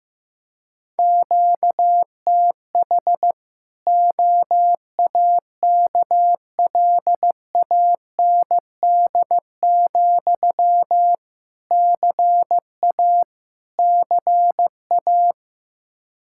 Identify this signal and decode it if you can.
Morse code, heard as a keyed tone, about 15 wpm, 700 Hz. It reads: QTH OAKLAND, CA CA
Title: Practice QSO - Exchange 1 - Beginning - 15wpm